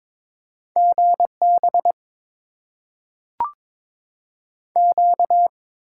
Morse code, heard as a keyed tone, about 21 wpm, 700 Hz. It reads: GB Q
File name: Top 90 QSO Elements - 6x Character Spacing - 22wpm